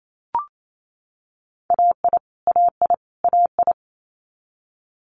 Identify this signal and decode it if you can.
Morse code, heard as a keyed tone, about 28 wpm, 700 Hz. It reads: AI AI AI
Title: Multi-letter Phonemes Repeated 3x 28wpm